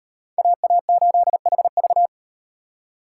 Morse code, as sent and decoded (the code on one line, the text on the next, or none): .- .- ---.. .... ...-
AA8HV